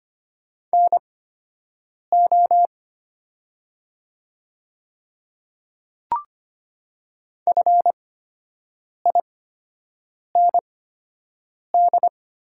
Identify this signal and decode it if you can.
Morse code, heard as a keyed tone, about 25 wpm, spaced as for 7 wpm, 700 Hz. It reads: NO FIND
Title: 8x Character Spacing - Top 100 Words - 25wpm